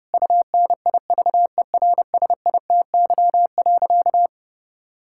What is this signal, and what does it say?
Morse code, heard as a keyed tone, about 32 wpm, 700 Hz. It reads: UNIVERSITY.
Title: Sentences from Top 1000 Words - Review Part 6 - 30wpm